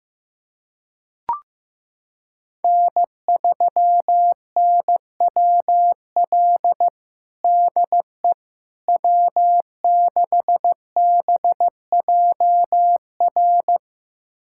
Morse code, heard as a keyed tone, about 15 wpm, 700 Hz. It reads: N3NWL DE W6BJR
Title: USA Callsign DE Callsign - 5 Characters - 15wpm